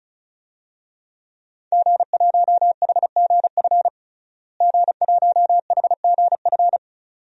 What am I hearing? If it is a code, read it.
Morse code, heard as a keyed tone, about 34 wpm, 700 Hz. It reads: G1HGF G1HGF
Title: DXCC-100 Call Signs - 5 Characters - Repeated 2x - 35wpm